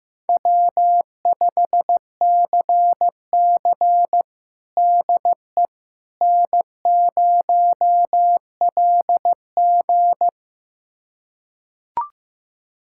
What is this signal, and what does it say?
Morse code, heard as a keyed tone, about 15 wpm, 700 Hz. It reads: W5CC DE N0LG
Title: USA Callsign DE Callsign - 4 Characters - 15wpm